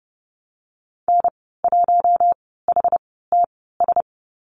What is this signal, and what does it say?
Morse code, heard as a keyed tone, about 30 wpm, spaced as for 17 wpm, 700 Hz. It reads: N1HTS
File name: Lesson 15g - Number 5 - Call Signs 30wpm